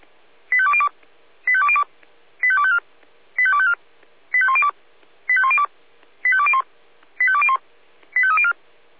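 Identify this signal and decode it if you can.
Five-tone selective call (ZVEI), heard as quick runs of five tones, tones of about 70 ms. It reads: 85202 85202 85254 85254 85102 85102 85201 85201 85304
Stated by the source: Five-tone ZVEI selective call 85296, received by SDR.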